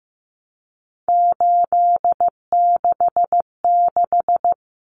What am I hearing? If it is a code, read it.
Morse code, heard as a keyed tone, about 15 wpm, 700 Hz. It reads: 866
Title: Common Cut Numbers - 3-Digit 15wpm